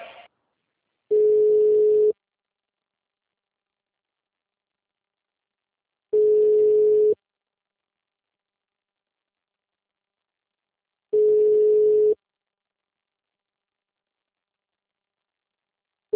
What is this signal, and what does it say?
Ringback tone (Europe, 425 Hz).
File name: out- -2003- - - .9257